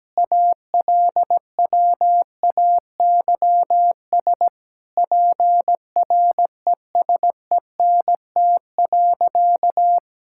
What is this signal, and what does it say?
Morse code, heard as a keyed tone, about 17 wpm, 700 Hz. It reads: ALWAYS PRESENT.
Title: Sentences from Top 1000 Words - Review Part 5 - 17wpm